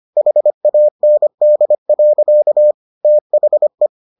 Morse code, heard as a keyed tone, about 25 wpm, 600 Hz. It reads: HAND. THE